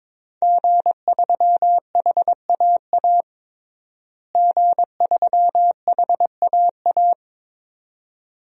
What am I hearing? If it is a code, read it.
Morse code, heard as a keyed tone, about 22 wpm, 700 Hz. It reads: G3HAA G3HAA
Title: DXCC-100 Call Signs - 5 Characters - Repeated 2x - 22wpm